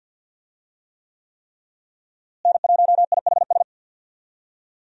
Morse code, heard as a keyed tone, about 51 wpm, 700 Hz. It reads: N1IHS
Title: USA Call Signs - 5 Characters - Repeated 3x - 50wpm